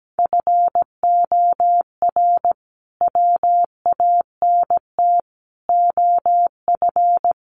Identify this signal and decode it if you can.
Morse code, heard as a keyed tone, about 17 wpm, 700 Hz. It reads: FOR WANT OF
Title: A Sound Of Thunder 17wpm